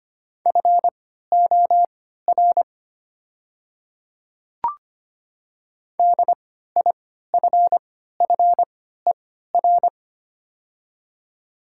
Morse code, heard as a keyed tone, about 25 wpm, spaced as for 14 wpm, 700 Hz. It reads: FOR DIFFER
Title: 3x Character Spacing - Top 200 Words - 25wpm